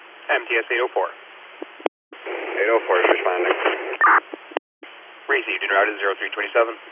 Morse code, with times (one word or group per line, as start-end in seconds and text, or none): none